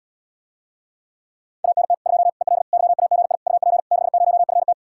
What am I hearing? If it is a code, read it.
Morse code, heard as a keyed tone, about 80 wpm, 700 Hz. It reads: ANT 2 EL YAGI UP 60FT